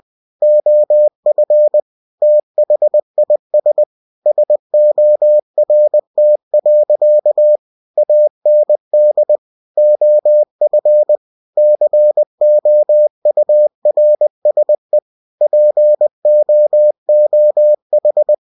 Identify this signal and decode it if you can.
Morse code, heard as a keyed tone, about 20 wpm, 600 Hz. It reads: OF THIS SORT. AND OF COURSE POOH
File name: cw-books / winnie-the-pooh / winnie-20wpm-page-0109